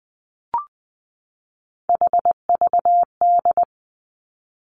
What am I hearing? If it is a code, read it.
Morse code, heard as a keyed tone, about 20 wpm, 700 Hz. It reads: HVD
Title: Three Challenging Letter-Number Combinations 20wpm